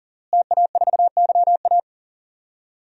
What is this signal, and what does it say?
Morse code, heard as a keyed tone, about 40 wpm, 700 Hz. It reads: TA4YA
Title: DXCC-100 Call Signs - 5 Characters - Repeated 3x - 40wpm